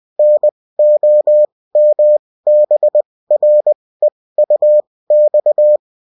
Morse code, heard as a keyed tone, about 20 wpm, spaced as for 16 wpm, 600 Hz. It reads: NOMBREUX